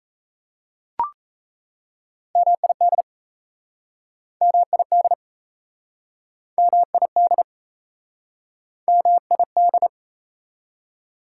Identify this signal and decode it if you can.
Morse code, heard as a keyed tone, about 38 wpm, 690 Hz. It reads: MID MID MID MID